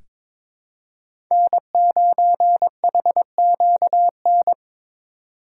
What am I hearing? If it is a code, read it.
Morse code, heard as a keyed tone, about 23 wpm, 700 Hz. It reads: N9HQN
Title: USA Call Signs - 5 Characters - 4x Character Spacing - 22wpm